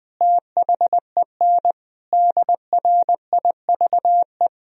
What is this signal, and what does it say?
Morse code, heard as a keyed tone, about 20 wpm, 700 Hz. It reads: THEN DRIVE
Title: Sentences from Top 1000 Words - Review Part 3 - 20wpm